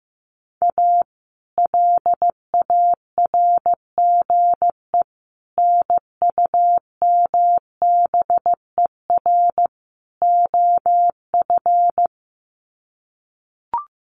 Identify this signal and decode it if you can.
Morse code, heard as a keyed tone, about 15 wpm, 700 Hz. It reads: A LARGE NUMBER OF
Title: Sets of 4 Words 15wpm